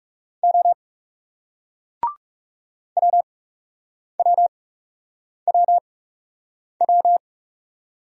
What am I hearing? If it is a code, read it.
Morse code, heard as a keyed tone, about 40 wpm, spaced as for 8 wpm, 700 Hz. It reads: O WWWW